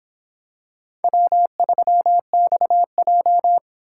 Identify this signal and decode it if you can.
Morse code, heard as a keyed tone, about 26 wpm, 700 Hz. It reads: W3XJ